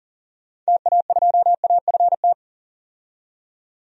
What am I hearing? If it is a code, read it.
Morse code, heard as a keyed tone, about 40 wpm, 700 Hz. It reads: TA2AFT
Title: DXCC-100 Call Signs - 6 Characters - 40wpm